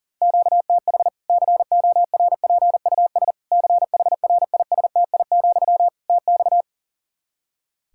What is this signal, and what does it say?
Morse code, heard as a keyed tone, about 37 wpm, 700 Hz. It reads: QTH CORPUS CHRISTI, TX
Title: Practice QSO - Exchange 1 - Beginning - 40wpm